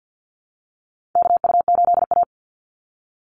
Morse code, heard as a keyed tone, about 48 wpm, 700 Hz. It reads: KU8A
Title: USA Call Signs 50wpm